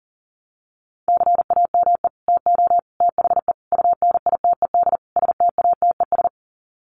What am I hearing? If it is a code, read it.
Morse code, heard as a keyed tone, about 40 wpm, 700 Hz. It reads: CAME TO THE UNITED STATES
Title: Sets of 5 Words Encore2 40wpm